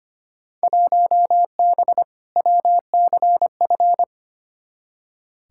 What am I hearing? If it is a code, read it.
Morse code, heard as a keyed tone, about 25 wpm, 700 Hz. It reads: 1B WCF
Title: Field Day Exchanges Encore - 25wpm